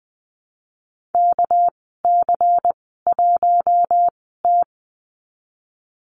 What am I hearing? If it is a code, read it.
Morse code, heard as a keyed tone, about 20 wpm, spaced as for 15 wpm, 700 Hz. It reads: KC1T
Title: 200 Lesson 32g - Letter k - Call Signs 20wpm